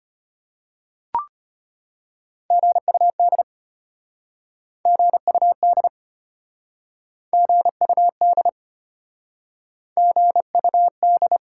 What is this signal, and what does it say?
Morse code, heard as a keyed tone, about 33 wpm, 700 Hz. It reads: GUD GUD GUD GUD